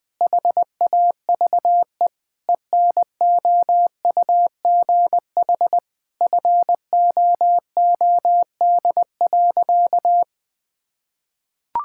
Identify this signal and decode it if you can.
Morse code, heard as a keyed tone, about 21 wpm, 700 Hz. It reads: HAVE ENOUGH FOOD.
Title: Sentences from Top 1000 Words - Review Part 5 - 20wpm